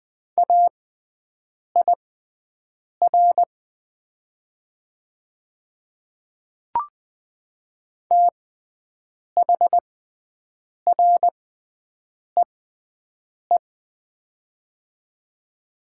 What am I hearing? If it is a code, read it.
Morse code, heard as a keyed tone, about 20 wpm, spaced as for 7 wpm, 700 Hz. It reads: AIR THREE
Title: 6x Character Spacing - Top 200 Words - 20wpm